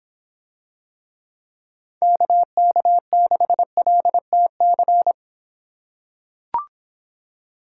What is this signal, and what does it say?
Morse code, heard as a keyed tone, about 26 wpm, 700 Hz. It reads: KK6LTC